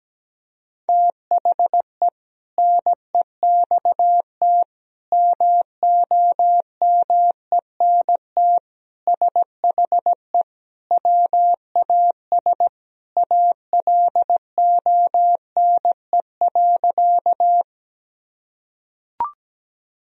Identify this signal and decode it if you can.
Morse code, heard as a keyed tone, about 17 wpm, 700 Hz. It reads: THE NEXT MOMENT SHE WAS ALONE.
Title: Sentences from Top 1000 Words - Review Part 1 - 17wpm